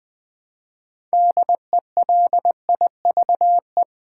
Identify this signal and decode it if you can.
Morse code, heard as a keyed tone, about 20 wpm, 700 Hz. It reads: DELIVE
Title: Build A Word - 2 Letters At A Time - 8-Letter Words - 20wpm